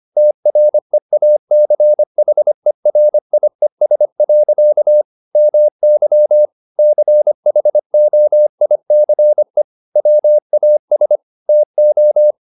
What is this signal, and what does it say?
Morse code, heard as a keyed tone, about 25 wpm, 600 Hz. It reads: TREACHERIES. MY CHOICE WAS TO
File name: the old man and the sea part 1 25wpm